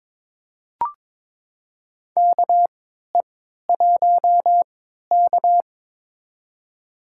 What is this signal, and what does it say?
Morse code, heard as a keyed tone, about 22 wpm, spaced as for 13 wpm, 700 Hz. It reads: KE1K
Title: USA Call Signs - 3x Character Spacing - 22wpm